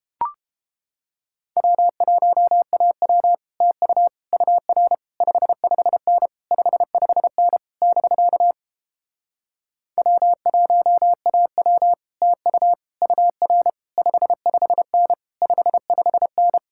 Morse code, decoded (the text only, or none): W1AW TU UR 55N 55N <BK> W1AW TU UR 55N 55N